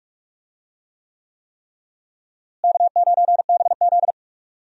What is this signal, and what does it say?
Morse code, heard as a keyed tone, about 43 wpm, 700 Hz. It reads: K9BZ
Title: USA Call Signs - 4 Characters - Repeated 3x - 45wpm